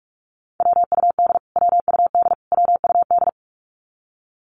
Morse code, heard as a keyed tone, about 44 wpm, 700 Hz. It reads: WUD WUD WUD
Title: Abbreviations Repeated 3x - 45wpm